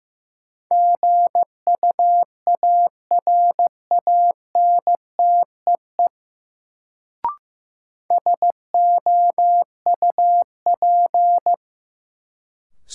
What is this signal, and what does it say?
Morse code, heard as a keyed tone, about 15 wpm, 700 Hz. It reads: GUARANTEE SOUP
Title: Sentences from Top 1600 Words - New Word Drill Alt - 15wpm